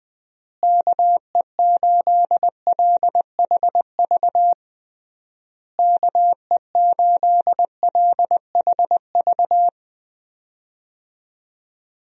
Morse code, heard as a keyed tone, about 20 wpm, 700 Hz. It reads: KE8LHV KE8LHV
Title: USA Call Signs - 6 Characters - Repeated 3x - 20wpm